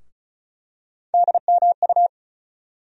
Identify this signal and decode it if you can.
Morse code, heard as a keyed tone, about 36 wpm, 700 Hz. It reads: DMU
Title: USA Call Sign Postfixes - 3 Letters 35wpm